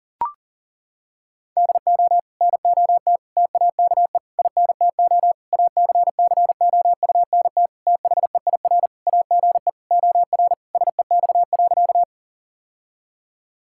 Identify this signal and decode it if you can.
Morse code, heard as a keyed tone, about 40 wpm, 700 Hz. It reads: DO NOT TAKE INTO ACCOUNT THEIR AGE OR SEX.
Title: Sentences from Top 1600 Words 40wpm